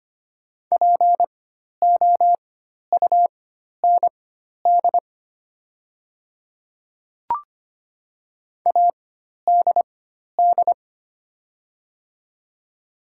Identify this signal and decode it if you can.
Morse code, heard as a keyed tone, about 25 wpm, spaced as for 12 wpm, 700 Hz. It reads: POUND ADD
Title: 4x Character Spacing - Top 500 Words - 25wpm